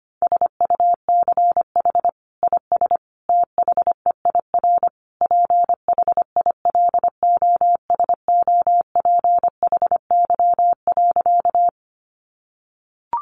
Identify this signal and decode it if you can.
Morse code, heard as a keyed tone, about 25 wpm, 700 Hz. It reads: SUCH IS THEIR PHILOSOPHY.
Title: Sentences from Top 900 Words 25wpm